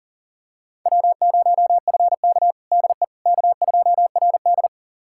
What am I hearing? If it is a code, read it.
Morse code, heard as a keyed tone, about 40 wpm, 700 Hz. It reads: W0FK DE K2RD
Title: USA Callsign DE Callsign - 4 Characters - 40wpm